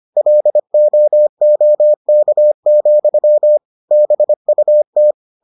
Morse code, heard as a keyed tone, about 25 wpm, 600 Hz. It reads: LOOK, BUT